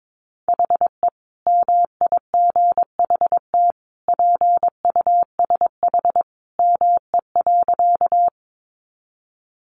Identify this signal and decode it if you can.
Morse code, heard as a keyed tone, about 22 wpm, 700 Hz. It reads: HE MIGHT PUSH ME.
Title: Sentences from Top 1300 Words 22wpm